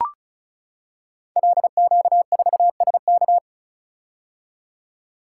Morse code, heard as a keyed tone, about 36 wpm, 700 Hz. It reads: LQ4SK
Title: Code Groups - 5 Letter:Numbers - 35wpm